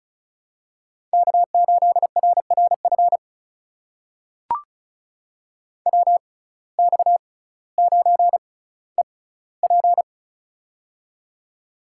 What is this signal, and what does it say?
Morse code, heard as a keyed tone, about 36 wpm, spaced as for 18 wpm, 700 Hz. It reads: K8RRF WX9EP
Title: USA Call Signs - 5 Characters - 6x Character Spacing - 35wpm